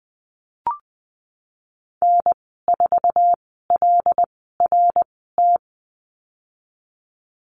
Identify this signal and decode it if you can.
Morse code, heard as a keyed tone, about 20 wpm, spaced as for 15 wpm, 700 Hz. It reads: N4LRT
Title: 154 Lesson 25g - Letter p - Call Signs 20wpm